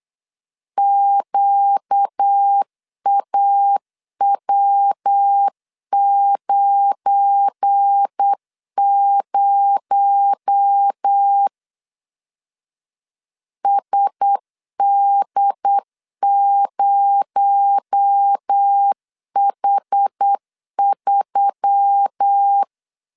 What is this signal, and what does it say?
Morse code, heard as a keyed tone, about 8 wpm, 800 Hz. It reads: QAW90 SD0H3
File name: LEZIONI / 7 GF9Z / 7 GF9 8wpm 3545